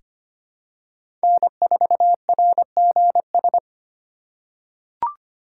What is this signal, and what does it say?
Morse code, heard as a keyed tone, about 25 wpm, 700 Hz. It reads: N4RGS
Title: USA Call Signs - 5 Characters - 2x Character Spacing - 25wpm